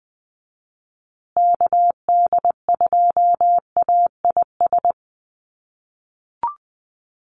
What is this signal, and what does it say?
Morse code, heard as a keyed tone, about 20 wpm, 700 Hz. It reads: KD2AIS
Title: USA Call Signs Repeated 2x 20wpm